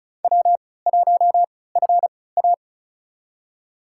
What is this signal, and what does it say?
Morse code, heard as a keyed tone, about 34 wpm, spaced as for 20 wpm, 700 Hz. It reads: W1FA
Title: USA Call Signs - 4 Characters - 3x Character Spacing - 35wpm